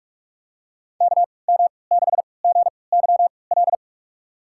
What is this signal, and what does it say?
Morse code, heard as a keyed tone, about 46 wpm, spaced as for 25 wpm, 700 Hz. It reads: KM6GYL
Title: USA Call Signs - 3x Character Spacing - 45wpm